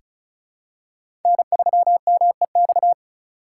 Morse code, heard as a keyed tone, about 36 wpm, 690 Hz. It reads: N3MEX
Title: USA Call Signs - 5 Characters - 3x Character Spacing - 35wpm